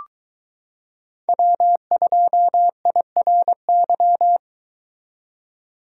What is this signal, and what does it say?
Morse code, heard as a keyed tone, about 22 wpm, 700 Hz. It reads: W2IRY